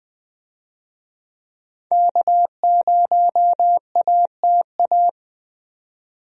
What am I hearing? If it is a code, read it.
Morse code, heard as a keyed tone, about 20 wpm, 700 Hz. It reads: K0ATA
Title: USA Call Signs - 5 Characters - 20wpm